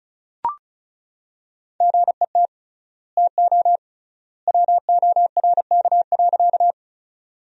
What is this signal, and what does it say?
Morse code, heard as a keyed tone, about 36 wpm, 700 Hz. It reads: GET TO WORK.
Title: 3x Word-Spacing - Sentences from Top 200 Words - 35wpm